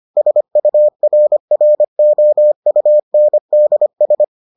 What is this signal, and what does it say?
Morse code, heard as a keyed tone, about 25 wpm, 600 Hz. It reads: SURROUNDS